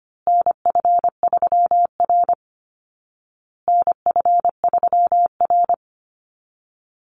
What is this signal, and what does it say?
Morse code, heard as a keyed tone, about 25 wpm, 700 Hz. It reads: NF3R NF3R
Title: Call Signs Repeated 2x - 25wpm